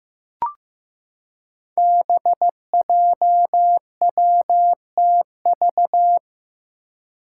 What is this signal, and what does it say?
Morse code, heard as a keyed tone, about 15 wpm, 700 Hz. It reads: BJWTV
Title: Five Challenging Letter-Number Combinations 15wpm